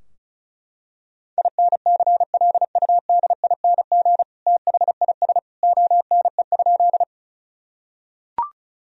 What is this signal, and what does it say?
Morse code, heard as a keyed tone, about 34 wpm, 700 Hz. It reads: INCLUDING THIS ONE?
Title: Sentences from Top 1000 Words - Review Part 4 - 35wpm